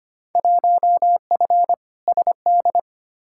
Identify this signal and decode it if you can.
Morse code, heard as a keyed tone, about 25 wpm, 700 Hz. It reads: 1F SD
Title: Field Day Exchanges - 25wpm